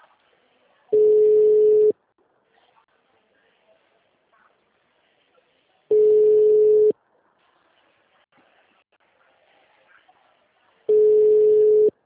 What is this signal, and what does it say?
Ringback tone (Europe, 425 Hz).